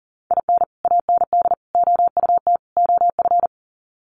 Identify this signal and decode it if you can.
Morse code, heard as a keyed tone, about 38 wpm, 700 Hz. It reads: IN AND OUT OF
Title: Sets of 4 Words 40wpm